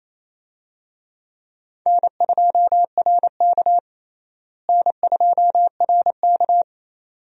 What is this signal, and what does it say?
Morse code, heard as a keyed tone, about 28 wpm, 700 Hz. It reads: N2RK N2RK
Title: USA Call Signs - 4 Characters - Repeated 2x - 28wpm